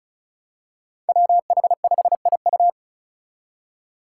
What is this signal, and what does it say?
Morse code, heard as a keyed tone, about 36 wpm, 700 Hz. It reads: WH5IU
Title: Five Challenging Letter-Number Combinations 35wpm